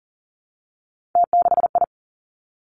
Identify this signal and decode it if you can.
Morse code, heard as a keyed tone, about 38 wpm, 690 Hz. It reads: T6I
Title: Three Challenging Letter-Number Combinations 40wpm